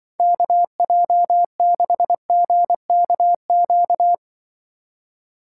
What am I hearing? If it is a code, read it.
Morse code, heard as a keyed tone, about 24 wpm, 700 Hz. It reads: KJ6GKQ
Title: Speed Racing - USA 2x3 Call Signs 20wpm